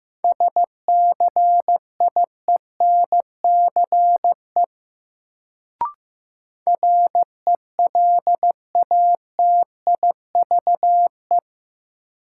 Morse code, heard as a keyed tone, about 15 wpm, 700 Hz. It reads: SCIENCE RELATIVE
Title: Sentences from Top 1400 Words - New Word Drill Alt - 15wpm